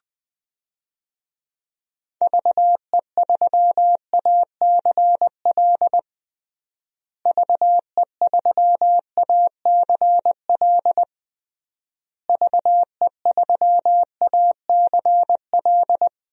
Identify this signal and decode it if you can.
Morse code, heard as a keyed tone, about 21 wpm, 700 Hz. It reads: VE3ACL VE3ACL VE3ACL
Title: DXCC-100 Call Signs - 6 Characters - Repeated 3x - 20wpm